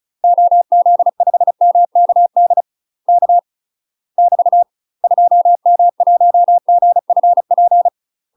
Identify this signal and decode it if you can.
Morse code, heard as a keyed tone, about 34 wpm, 700 Hz. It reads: OZ5MKD K <BT> 2M1GFP
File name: UK MASTER QSOs TRAINING / 35WPM / UK MASTER QSOs 0009